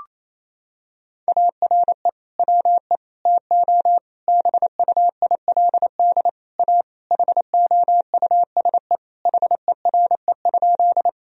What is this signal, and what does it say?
Morse code, heard as a keyed tone, about 28 wpm, 700 Hz. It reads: ARE WE TO BUILD A HOUSE HERE?
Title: Sentences from Top 1000 Words - Review Part 1 - 28wpm